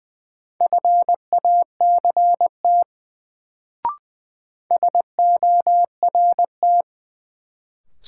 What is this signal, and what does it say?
Morse code, heard as a keyed tone, about 20 wpm, 700 Hz. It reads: FACT SORT
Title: Sets of 2 Words - New Word Drill - 20wpm